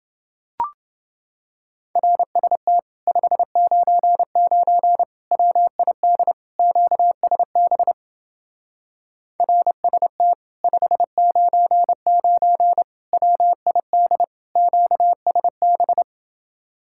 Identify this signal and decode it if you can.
Morse code, heard as a keyed tone, about 31 wpm, 700 Hz. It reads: RST 599 WID QSB RST 599 WID QSB